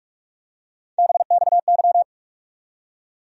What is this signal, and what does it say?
Morse code, heard as a keyed tone, about 42 wpm, 700 Hz. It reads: BXY
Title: USA Call Sign Postfixes - 3 Letters 45wpm